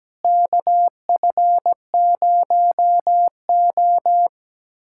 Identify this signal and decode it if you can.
Morse code, heard as a keyed tone, about 17 wpm, 700 Hz. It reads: KF0O
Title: USA Callsign DE Callsign - 4 Characters - 17wpm